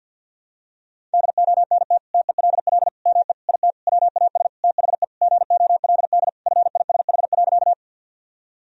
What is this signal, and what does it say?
Morse code, heard as a keyed tone, about 51 wpm, 700 Hz. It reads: DONT TELL ME IT WAS THE GOLD FISH.
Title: Sentences from Top 1000 Words - Review Part 3 - 50wpm